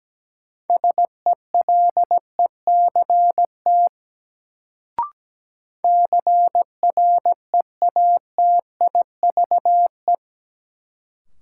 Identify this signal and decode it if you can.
Morse code, heard as a keyed tone, about 17 wpm, 700 Hz. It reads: SELECT CREATIVE
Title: Sentences from Top 1300 Words - New Word Drill Alt - 17wpm